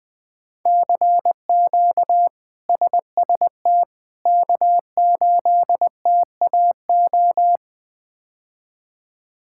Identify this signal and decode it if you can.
Morse code, heard as a keyed tone, about 19 wpm, 700 Hz. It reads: CQ SST K8TAO
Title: K1USN SST QSO Practice - 20wpm